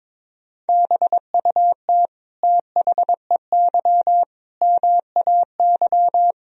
Morse code, heard as a keyed tone, about 22 wpm, 700 Hz. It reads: BUT THEY MAY
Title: Sentences from Top 300 Words and Abbreviations 22wpm